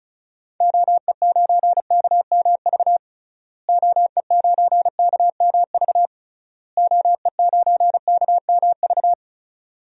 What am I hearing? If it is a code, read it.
Morse code, heard as a keyed tone, about 36 wpm, 700 Hz. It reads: OE9KMV OE9KMV OE9KMV
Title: DXCC-100 Call Signs - 6 Characters - Repeated 3x - 35wpm